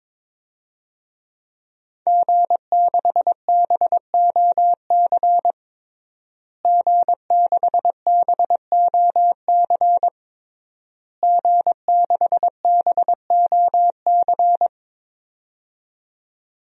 Morse code, heard as a keyed tone, about 22 wpm, 700 Hz. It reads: G6BOC G6BOC G6BOC
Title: DXCC-100 Call Signs - 5 Characters - Repeated 3x - 22wpm